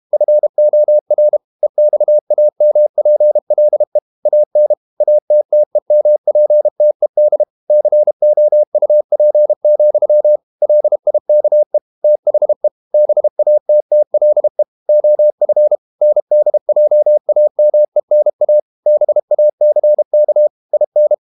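Morse code, heard as a keyed tone, about 32 wpm, 600 Hz. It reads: FOR EXAMPLE AN ATTEMPTED COUP, LIKE THE BATTLE OF NDJAMENA BACK IN
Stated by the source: Morse code practice at 32 wpm from CW med Gnister, reading Ken Follett's novel Never in English.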